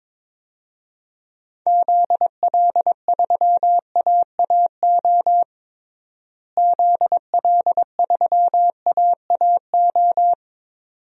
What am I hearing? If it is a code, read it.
Morse code, heard as a keyed tone, about 22 wpm, 700 Hz. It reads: ZL3AAO ZL3AAO
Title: DXCC-100 Call Signs - 6 Characters - Repeated 2x - 22wpm